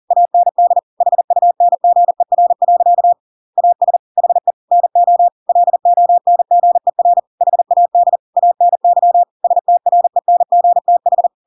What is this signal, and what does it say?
Morse code, heard as a keyed tone, about 40 wpm, 700 Hz. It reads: AND HUNGER. AS HE NO LONGER HAD ANY STRENGTH
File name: Pinocchio / Pinocchio ch6 40 40 ik1hge